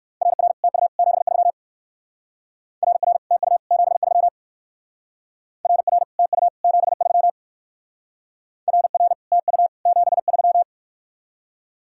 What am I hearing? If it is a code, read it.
Morse code, heard as a keyed tone, about 57 wpm, 700 Hz. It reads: RR TU 73 RR TU 73 RR TU 73 RR TU 73